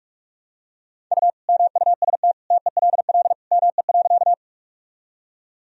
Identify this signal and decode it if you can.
Morse code, heard as a keyed tone, about 42 wpm, 700 Hz. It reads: U MUST TELL ME.
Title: Sentences from Top 200 Words and Abbreviations 45wpm